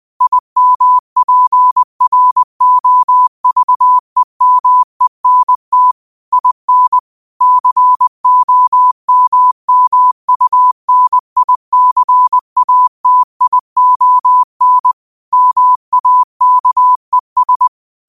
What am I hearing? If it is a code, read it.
Morse code, heard as a keyed tone, about 20 wpm, 1000 Hz. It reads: IMPROVEMENT IN COMMUNICATION MAKES
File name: qotdfiles / 20wpm